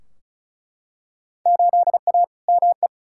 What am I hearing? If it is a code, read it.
Morse code, heard as a keyed tone, about 34 wpm, 700 Hz. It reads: 8A ME
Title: Field Day Exchanges - 35wpm